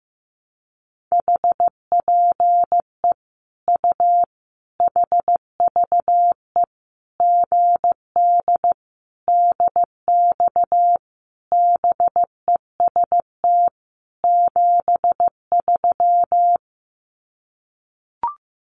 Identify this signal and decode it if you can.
Morse code, heard as a keyed tone, about 15 wpm, 700 Hz. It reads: HPE U HVE GD DX BEST 73
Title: Common QSO Phrases 15wpm